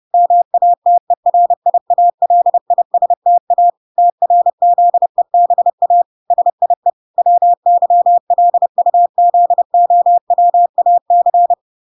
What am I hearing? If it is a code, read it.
Morse code, heard as a keyed tone, about 30 wpm, 700 Hz. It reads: MATERIALISTA TRZEBA SIE WYLUZOWAC